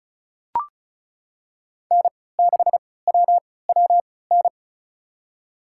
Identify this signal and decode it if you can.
Morse code, heard as a keyed tone, about 34 wpm, spaced as for 20 wpm, 690 Hz. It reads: N6WWN
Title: USA Call Signs - 3x Character Spacing - 35wpm